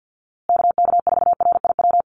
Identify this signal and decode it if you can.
Morse code, heard as a keyed tone, about 50 wpm, 700 Hz. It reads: KK4RIW
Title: USA Call Signs Repeated 3x 50wpm